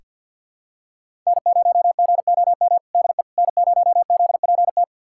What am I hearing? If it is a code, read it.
Morse code, heard as a keyed tone, about 47 wpm, 700 Hz. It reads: N0GOM DE N0ZPT
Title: USA Callsign DE Callsign - 5 Characters - 50wpm